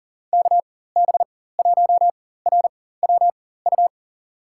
Morse code, heard as a keyed tone, about 40 wpm, spaced as for 19 wpm, 700 Hz. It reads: KB1RWU
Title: USA Call Signs - 6 Characters - 4x Character Spacing - 40wpm